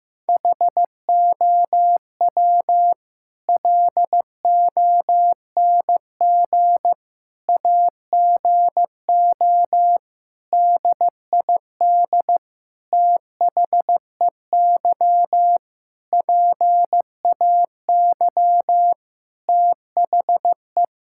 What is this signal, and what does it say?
Morse code, heard as a keyed tone, about 15 wpm, 700 Hz. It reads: HOW LONG AGO DID THEY PAY THE
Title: Sentences from Top 1000 Words - Review Part 5 - 15wpm